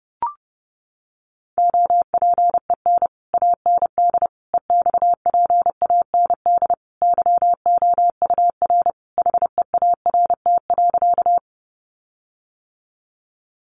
Morse code, heard as a keyed tone, about 30 wpm, 700 Hz. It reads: OPEN AND EXPAND YOUR HEART.
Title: Sentences from Top 1400 Words 30wpm